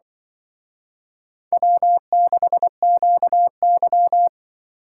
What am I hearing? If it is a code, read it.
Morse code, heard as a keyed tone, about 24 wpm, 700 Hz. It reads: W6QY
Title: Speed Racing - USA 1x2 Call Signs 20wpm